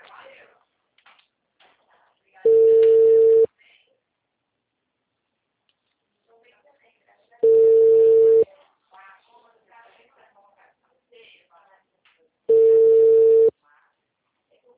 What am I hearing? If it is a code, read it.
Ringback tone (China, 450 Hz), with speech.